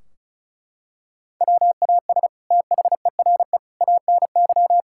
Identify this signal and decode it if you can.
Morse code, heard as a keyed tone, about 36 wpm, 700 Hz. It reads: WAS THERE ANY
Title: Sentences from Top 1600 Words 35wpm